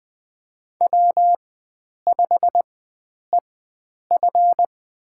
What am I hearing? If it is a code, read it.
Morse code, heard as a keyed tone, about 20 wpm, spaced as for 9 wpm, 700 Hz. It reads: W5EF
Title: USA Call Signs - 4x Character Spacing - 20wpm